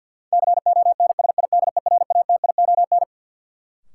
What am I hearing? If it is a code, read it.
Morse code, heard as a keyed tone, about 53 wpm, 700 Hz. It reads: CONSIDERATION
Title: Sentences from Top 1500 Words - New Word Drill - 50wpm